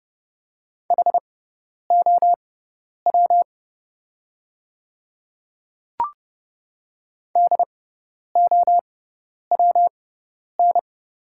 Morse code, heard as a keyed tone, about 32 wpm, spaced as for 10 wpm, 700 Hz. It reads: HOW DOWN
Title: 6x Character Spacing - Top 200 Words - 30wpm